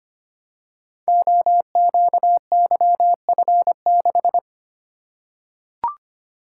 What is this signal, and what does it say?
Morse code, heard as a keyed tone, about 25 wpm, 700 Hz. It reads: OQYF6
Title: Code Groups - 5 Letter:Numbers - 25wpm